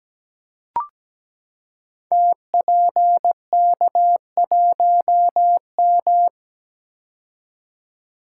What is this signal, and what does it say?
Morse code, heard as a keyed tone, about 17 wpm, 700 Hz. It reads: TPK1M
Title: Code Groups - 5 Letter:Numbers - 17wpm